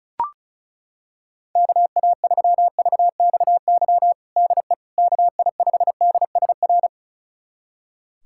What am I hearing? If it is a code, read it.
Morse code, heard as a keyed tone, about 36 wpm, 700 Hz. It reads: KA3VXY DE KI5DSR
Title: USA Callsign DE Callsign 35wpm